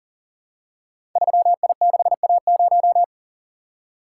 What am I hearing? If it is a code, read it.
Morse code, heard as a keyed tone, about 38 wpm, 700 Hz. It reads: 3I6A0
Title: Code Groups - 5 Letter:Numbers - 40wpm